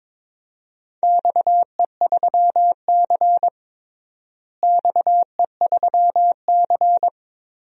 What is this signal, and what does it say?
Morse code, heard as a keyed tone, about 22 wpm, 700 Hz. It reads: XE3C XE3C
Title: DXCC-100 Call Signs - 4 Characters - Repeated 3x - 22wpm